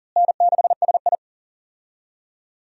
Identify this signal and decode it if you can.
Morse code, heard as a keyed tone, about 40 wpm, 700 Hz. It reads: N6SI
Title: USA Call Signs - 4 Characters - Repeated 3x - 40wpm